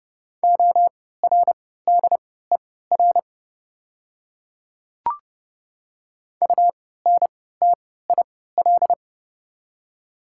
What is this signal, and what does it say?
Morse code, heard as a keyed tone, about 30 wpm, spaced as for 17 wpm, 700 Hz. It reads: ORDER UNTIL
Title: 3x Character Spacing - Top 500 Words - 30wpm